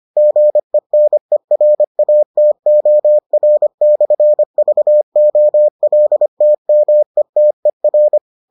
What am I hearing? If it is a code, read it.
Morse code, heard as a keyed tone, about 25 wpm, 600 Hz. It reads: GENERATOR/VOLTMETER